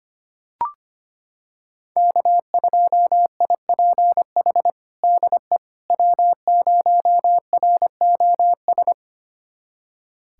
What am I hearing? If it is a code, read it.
Morse code, heard as a keyed tone, about 25 wpm, 700 Hz. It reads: K2IPH DE W0ROS
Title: USA Callsign DE Callsign - 5 Characters - 25wpm